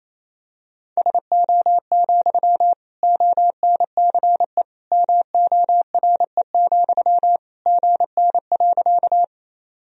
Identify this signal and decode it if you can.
Morse code, heard as a keyed tone, about 28 wpm, 700 Hz. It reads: SO, ONCE MORE, GN.
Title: Sentences from Top 300 Words and Abbreviations 28wpm